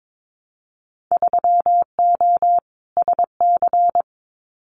Morse code, heard as a keyed tone, about 22 wpm, 700 Hz. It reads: 3O SC
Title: Winter Field Day Exchanges Encore 22wpm